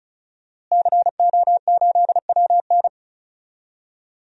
Morse code, heard as a keyed tone, about 34 wpm, 700 Hz. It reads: CO8WN
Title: DXCC-100 Call Signs - 5 Characters - Repeated 2x - 35wpm